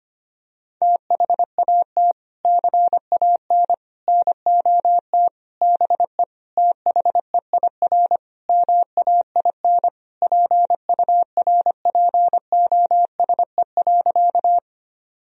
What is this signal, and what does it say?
Morse code, heard as a keyed tone, about 25 wpm, 700 Hz. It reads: THAT CAN NOT BE THEIR MAIN PURPOSE.
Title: Sentences from Top 1000 Words - Review Part 2 - 25wpm